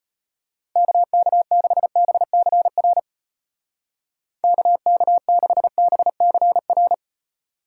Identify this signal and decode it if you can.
Morse code, heard as a keyed tone, about 31 wpm, 700 Hz. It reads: KK6BCR KK6BCR